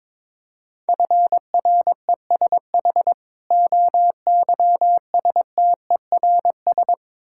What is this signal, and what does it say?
Morse code, heard as a keyed tone, about 21 wpm, 690 Hz. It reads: FRESH OYSTERS
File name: Difficult Words with Hints - 22wpm